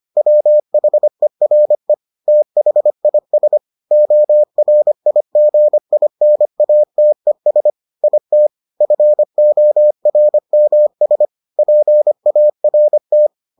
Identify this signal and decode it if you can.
Morse code, heard as a keyed tone, about 25 wpm, 600 Hz. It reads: WHERE THIS ORIGINATES IT FORMS PART